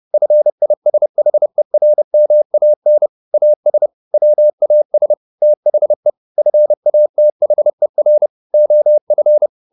Morse code, heard as a keyed tone, about 30 wpm, 600 Hz. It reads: FISHERMAN AS WAS THE FATHER OF